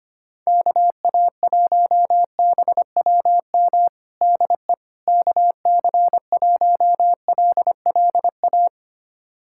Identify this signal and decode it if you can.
Morse code, heard as a keyed tone, about 25 wpm, 700 Hz. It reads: KA1BWM DE KC1LLA
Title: USA Callsign DE Callsign 25wpm